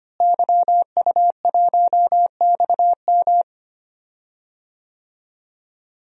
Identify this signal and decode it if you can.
Morse code, heard as a keyed tone, about 25 wpm, 700 Hz. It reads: YU1XM
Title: DXCC-100 Call Signs - 5 Characters - Repeated 3x - 25wpm